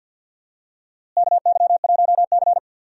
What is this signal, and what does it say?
Morse code, heard as a keyed tone, about 51 wpm, 700 Hz. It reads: KY1C
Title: USA Call Signs - 4 Characters - Repeated 2x - 50wpm